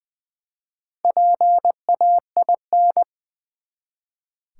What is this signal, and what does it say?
Morse code, heard as a keyed tone, about 21 wpm, 700 Hz. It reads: PAIN
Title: Four Letter Words - 20wpm